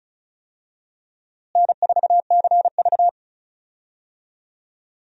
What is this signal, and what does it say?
Morse code, heard as a keyed tone, about 36 wpm, 700 Hz. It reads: N4CV
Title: USA Call Signs - 4 Characters - Repeated 3x - 35wpm